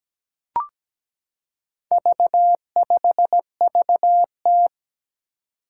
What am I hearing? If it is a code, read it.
Morse code, heard as a keyed tone, about 17 wpm, 700 Hz. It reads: V5VT
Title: Four Challenging Letter-Number Combinations 17wpm